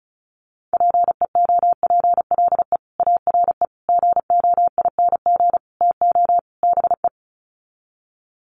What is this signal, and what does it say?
Morse code, heard as a keyed tone, about 34 wpm, 700 Hz. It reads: PEOPLE ARE GOING TO BE
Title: Sets of 5 Words Encore 35wpm